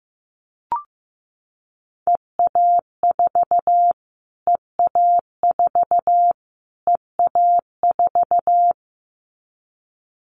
Morse code, heard as a keyed tone, about 15 wpm, 700 Hz. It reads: EA4 EA4 EA4
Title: Callsign Prefixes for Top 25 DXCC - Repeated 3x - 15wpm